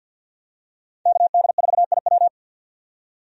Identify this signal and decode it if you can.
Morse code, heard as a keyed tone, about 51 wpm, 700 Hz. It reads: KD4IW
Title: USA Call Signs - 5 Characters - 50wpm